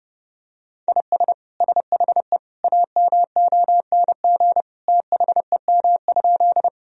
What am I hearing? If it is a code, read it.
Morse code, heard as a keyed tone, about 30 wpm, 700 Hz. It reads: IS SHE AMONG THEM?
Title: Sentences from Top 1000 Words - Review Part 3 - 30wpm